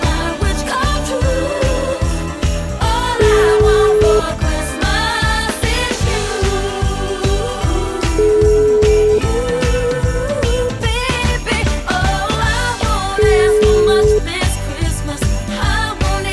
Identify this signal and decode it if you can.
Ringback tone (Europe, 425 Hz), with noise.